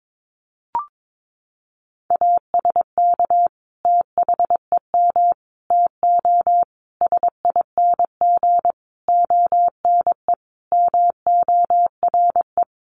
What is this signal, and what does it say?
Morse code, heard as a keyed tone, about 22 wpm, 700 Hz. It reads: ASK THEM TO SING ONE MORE
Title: Sentences from Top 400 Words 22wpm